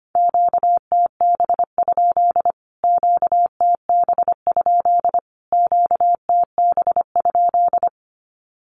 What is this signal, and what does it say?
Morse code, heard as a keyed tone, about 25 wpm, 700 Hz. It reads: QTB? QTB? QTB?
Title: Q-Codes Repeated 3x 25wpm